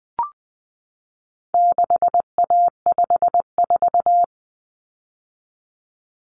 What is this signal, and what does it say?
Morse code, heard as a keyed tone, about 20 wpm, 700 Hz. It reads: 6A54
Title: Common Cut Numbers - 4-Digit 20wpm